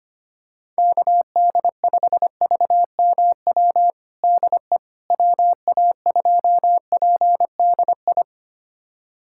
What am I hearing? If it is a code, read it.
Morse code, heard as a keyed tone, about 25 wpm, 700 Hz. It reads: KD5VMW DE WA2PDI
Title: USA Callsign DE Callsign - 6 Characters - 25wpm